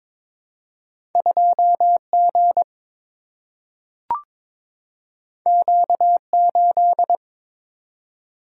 Morse code, heard as a keyed tone, about 21 wpm, 700 Hz. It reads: 2 G Q 8
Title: Single-Letter Single-Number 22wpm